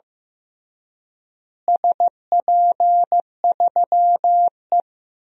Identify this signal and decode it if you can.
Morse code, heard as a keyed tone, about 16 wpm, 700 Hz. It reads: SP3E
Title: DXCC-100 Call Signs - 4 Characters - Repeated 3x - 15wpm